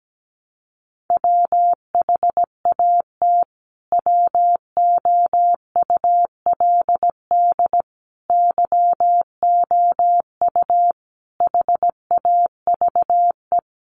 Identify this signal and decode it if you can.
Morse code, heard as a keyed tone, about 17 wpm, 700 Hz. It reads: WHAT WOULD YOU HAVE
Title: Sentences from Top 300 Words 17wpm